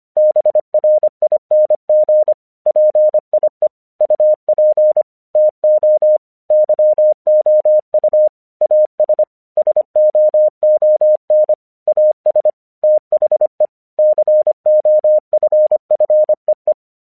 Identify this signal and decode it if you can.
Morse code, heard as a keyed tone, about 25 wpm, 610 Hz. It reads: BRING PIE UP TO YOU AS SOON AS THE COFFEE
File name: 1 English 25wpm / AQC22